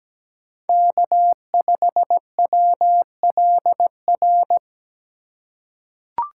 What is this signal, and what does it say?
Morse code, heard as a keyed tone, about 17 wpm, 700 Hz. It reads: K5WLR
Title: USA Call Signs - 5 Characters - 4x Character Spacing - 17wpm